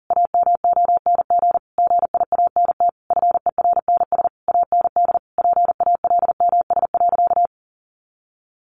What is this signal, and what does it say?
Morse code, heard as a keyed tone, about 40 wpm, 700 Hz. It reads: AMONG GIANT FERNS AND PALMS.
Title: A Sound Of Thunder 40wpm